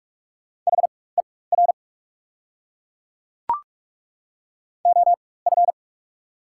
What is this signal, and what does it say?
Morse code, heard as a keyed tone, about 42 wpm, spaced as for 21 wpm, 690 Hz. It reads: HER OF
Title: 4x Character Spacing - Top 300 Words - 45wpm